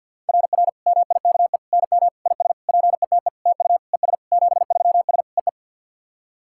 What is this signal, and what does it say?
Morse code, heard as a keyed tone, about 50 wpm, 690 Hz. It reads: RR MIKE NM IS PETE TU ES 73S EE
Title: SOTA QSO Practice 50wpm